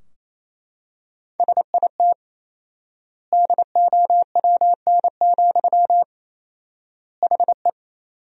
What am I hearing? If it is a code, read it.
Morse code, heard as a keyed tone, about 28 wpm, 700 Hz. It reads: SIT DOWN, HE
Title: 4x Word-Spacing - Sentences from Top 500 Words - 28wpm